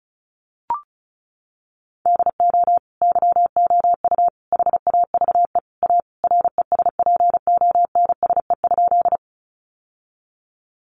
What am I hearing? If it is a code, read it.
Morse code, heard as a keyed tone, about 34 wpm, 700 Hz. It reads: DO YOU HAVE A RESPONSE?
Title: Sentences from Top 1400 Words 35wpm